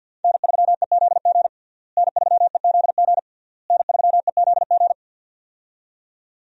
Morse code, heard as a keyed tone, about 48 wpm, 700 Hz. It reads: N3EZG N3EZG N3EZG
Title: USA Call Signs - 5 Characters - Repeated 3x - 50wpm